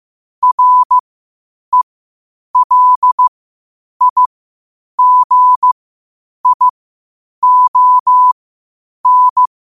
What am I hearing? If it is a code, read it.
Morse code, heard as a keyed tone, about 15 wpm, spaced as for 8 wpm, 1000 Hz. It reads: RELIGION
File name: qotdfiles / 8wpm